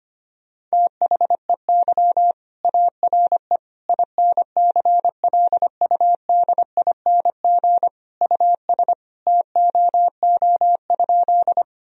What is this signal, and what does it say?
Morse code, heard as a keyed tone, about 25 wpm, 700 Hz. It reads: THEY ARE INCLUDING US TOO?
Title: Sentences from Top 1000 Words - Review Part 4 - 25wpm